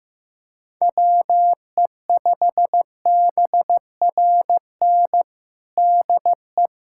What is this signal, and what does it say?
Morse code, heard as a keyed tone, about 15 wpm, 700 Hz. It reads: WE5BRN DE
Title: USA Callsign DE Callsign - 6 Characters - 15wpm